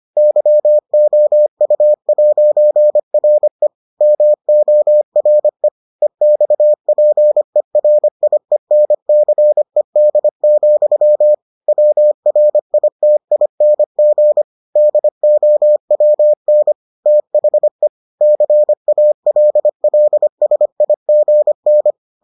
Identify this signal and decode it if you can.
Morse code, heard as a keyed tone, about 25 wpm, 600 Hz. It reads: YOU'RE MORE EXPERIENCED, WRITING DOWN THE CALLSIGN